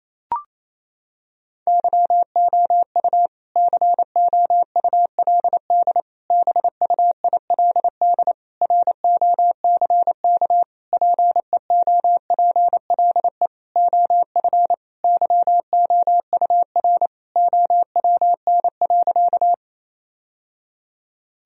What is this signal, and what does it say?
Morse code, heard as a keyed tone, about 28 wpm, 700 Hz. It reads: YOU COULD BUILD ROCK PEOPLE OF YOUR OWN.
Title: Sentences from Top 1000 Words - Review Part 2 - 28wpm